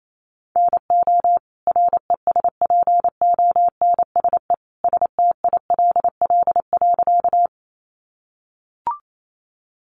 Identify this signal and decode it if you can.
Morse code, heard as a keyed tone, about 28 wpm, 700 Hz. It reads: NO RESPONSE STILL.
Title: Sentences from Top 1400 Words 28wpm